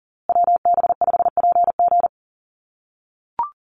Morse code, heard as a keyed tone, about 40 wpm, 700 Hz. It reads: WB5PG
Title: USA Call Signs Repeated 2x 40wpm